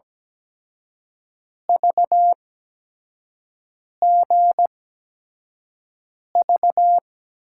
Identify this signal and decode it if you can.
Morse code, heard as a keyed tone, about 17 wpm, spaced as for 5 wpm, 700 Hz. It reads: VGV
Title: USA Call Signs - 5 Characters - 8x Character Spacing - 17wpm